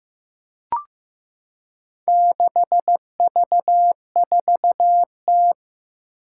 Morse code, heard as a keyed tone, about 15 wpm, 700 Hz. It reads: 6V4T
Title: Four Challenging Letter-Number Combinations 15wpm